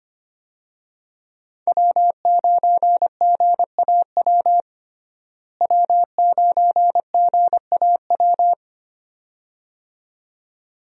Morse code, heard as a keyed tone, about 25 wpm, 700 Hz. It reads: W9GAW W9GAW
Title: USA Call Signs - 5 Characters - Repeated 2x - 25wpm